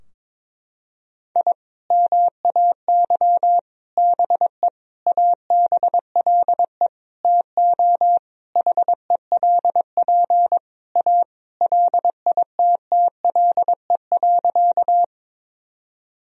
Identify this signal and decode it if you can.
Morse code, heard as a keyed tone, about 22 wpm, 700 Hz. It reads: I MAY BE ABLE TO HELP A LITTLE.
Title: Sentences from Top 1000 Words - Review Part 2 - 22wpm